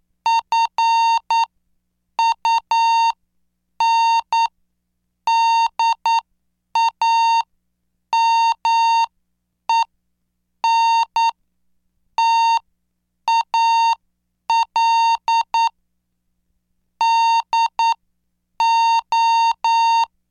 Morse code, heard as a keyed tone, about 9 wpm, spaced as for 7 wpm, 920 Hz. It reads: FUNDAMENTAL DO